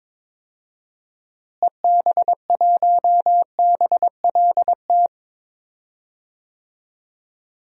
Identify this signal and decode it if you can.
Morse code, heard as a keyed tone, about 21 wpm, 700 Hz. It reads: EB1BLT
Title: DXCC-100 Call Signs - 6 Characters - Repeated 3x - 22wpm